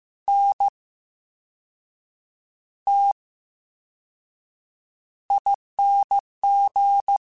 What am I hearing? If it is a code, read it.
Morse code, heard as a keyed tone, about 15 wpm, spaced as for 6 wpm, 800 Hz. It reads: NTING
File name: Special INGs / Special INGs-mp3s / Slow ing @13